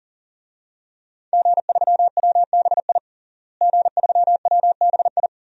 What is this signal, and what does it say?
Morse code, heard as a keyed tone, about 37 wpm, 700 Hz. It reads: G3WBI G3WBI
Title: DXCC-100 Call Signs - 5 Characters - Repeated 3x - 40wpm